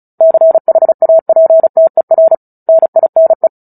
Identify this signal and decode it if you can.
Morse code, heard as a keyed tone, about 34 wpm, 650 Hz. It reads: CHAPTER NINE
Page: War of the Worlds - 09-Chapter 9 - 35 WPM